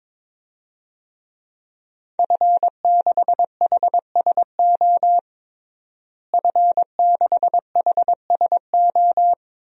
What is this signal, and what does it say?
Morse code, heard as a keyed tone, about 22 wpm, 700 Hz. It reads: F6HSO F6HSO
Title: DXCC-100 Call Signs - 5 Characters - Repeated 3x - 22wpm